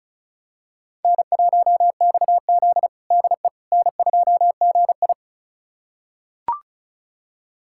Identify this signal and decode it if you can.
Morse code, heard as a keyed tone, about 36 wpm, 700 Hz. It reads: N1XZ DE N2GI
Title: USA Callsign DE Callsign - 4 Characters - 35wpm